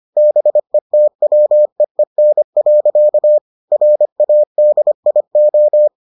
Morse code, heard as a keyed tone, about 25 wpm, 600 Hz. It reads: BETWEEN. RADIO